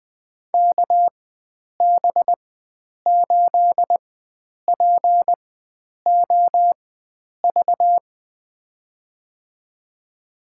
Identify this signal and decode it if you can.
Morse code, heard as a keyed tone, about 20 wpm, spaced as for 9 wpm, 700 Hz. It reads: KB8POV
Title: USA Call Signs - 6 Characters - 4x Character Spacing - 20wpm